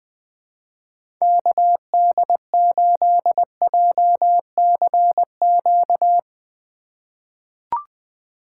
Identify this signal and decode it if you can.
Morse code, heard as a keyed tone, about 20 wpm, 700 Hz. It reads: KD8JCQ
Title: USA Call Signs - 4x Character Spacing - 20wpm